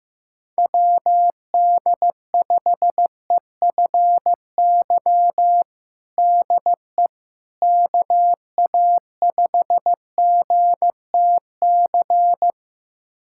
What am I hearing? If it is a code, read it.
Morse code, heard as a keyed tone, about 16 wpm, 700 Hz. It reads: WD5EFY DE KA5GTC
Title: USA Callsign DE Callsign - 6 Characters - 15wpm